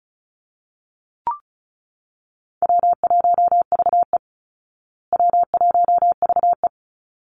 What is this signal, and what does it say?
Morse code, heard as a keyed tone, about 34 wpm, 700 Hz. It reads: W1VE W1VE
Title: Easy Call Signs Repeated 2x - 35wpm